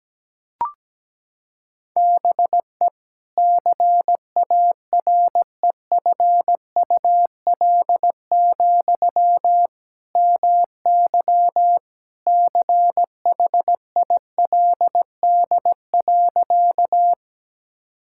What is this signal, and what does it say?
Morse code, heard as a keyed tone, about 17 wpm, 700 Hz. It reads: BE CAREFUL, MY CHILD.
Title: Sentences from Top 1600 Words 17wpm